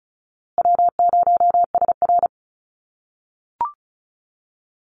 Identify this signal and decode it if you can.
Morse code, heard as a keyed tone, about 34 wpm, 700 Hz. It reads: W0SR
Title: USA Call Signs Repeated 3x 35wpm